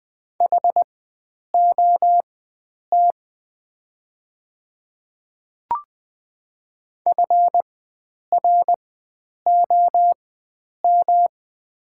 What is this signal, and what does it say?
Morse code, heard as a keyed tone, about 20 wpm, spaced as for 9 wpm, 690 Hz. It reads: HOT FROM
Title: 4x Character Spacing - Top 100 Words - 20wpm